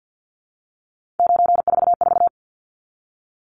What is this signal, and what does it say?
Morse code, heard as a keyed tone, about 50 wpm, 700 Hz. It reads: 944
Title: Numbers - 3-Digit 50wpm